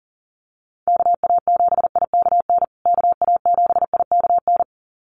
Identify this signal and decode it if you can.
Morse code, heard as a keyed tone, about 40 wpm, 700 Hz. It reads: KA7IKN KA7IKN
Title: USA Call Signs Repeated 2x 40wpm